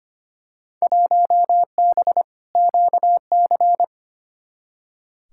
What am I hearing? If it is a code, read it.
Morse code, heard as a keyed tone, about 25 wpm, 700 Hz. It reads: 1B QC
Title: Field Day Exchanges - 25wpm